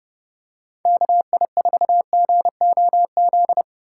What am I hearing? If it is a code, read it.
Morse code, heard as a keyed tone, about 30 wpm, 700 Hz. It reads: KI4GOZ